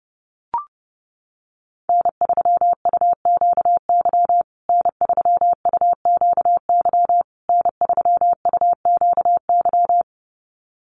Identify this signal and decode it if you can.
Morse code, heard as a keyed tone, about 30 wpm, 700 Hz. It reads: N3UQY N3UQY N3UQY
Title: USA Call Signs Repeated 3x 30wpm